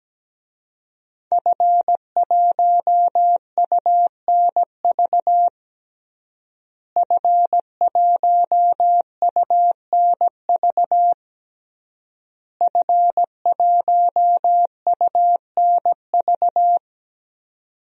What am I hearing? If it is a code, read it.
Morse code, heard as a keyed tone, about 17 wpm, 700 Hz. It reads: F1UNV F1UNV F1UNV
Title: DXCC-100 Call Signs - 5 Characters - Repeated 3x - 17wpm